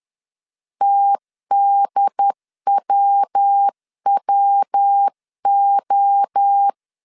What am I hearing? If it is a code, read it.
Morse code, heard as a keyed tone, about 10 wpm, 800 Hz. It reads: TDWWO
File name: CW / DET / 1 3 1500C S A 10wpm 3447